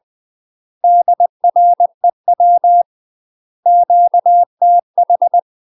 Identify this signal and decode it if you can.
Morse code, heard as a keyed tone, about 20 wpm, 700 Hz. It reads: DREW QTH
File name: UK MASTER QSOs TRAINING / 20WPM / UK MASTER QSOs 0050